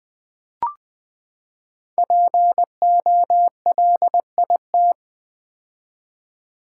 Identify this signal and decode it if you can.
Morse code, heard as a keyed tone, about 20 wpm, 700 Hz. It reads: POLIT
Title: Build A Word - 5 Letters At A Time - 10-Letter Words - 20wpm